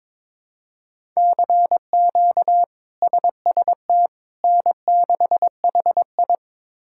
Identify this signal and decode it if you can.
Morse code, heard as a keyed tone, about 21 wpm, 700 Hz. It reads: CQ SST N6HI
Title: K1USN SST QSO Practice - 22wpm